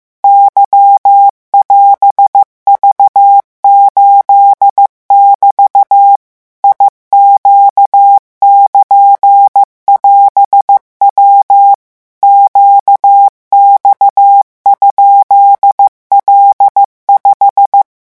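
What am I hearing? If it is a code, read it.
Morse code, heard as a keyed tone, about 15 wpm, 800 Hz. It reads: Y<AS>V8<BT> IQ<KN><AS>W QX?L5
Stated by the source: Morse revision lesson at 14 words per minute.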